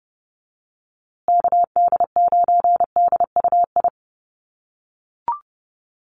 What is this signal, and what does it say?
Morse code, heard as a keyed tone, about 29 wpm, 700 Hz. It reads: KD9DUI
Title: USA Call Signs Repeated 3x 30wpm